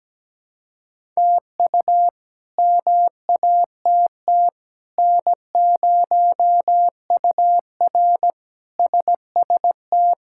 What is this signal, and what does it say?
Morse code, heard as a keyed tone, about 17 wpm, 700 Hz. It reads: TU MATT N0UR SST
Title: K1USN SST QSO Practice - 17wpm